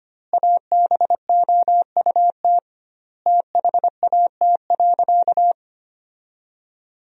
Morse code, heard as a keyed tone, about 25 wpm, 700 Hz. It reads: ABOUT THAT.
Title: 2x Word-Spacing - Sentences from Top 400 Words - 25wpm